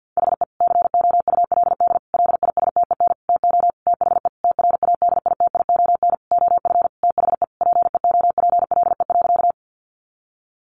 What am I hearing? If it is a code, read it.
Morse code, heard as a keyed tone, about 50 wpm, 700 Hz. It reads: HE COULD LISTEN TO THE TRADITION OF THE PEOPLE.
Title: Sentences from Top 1500 Words 50wpm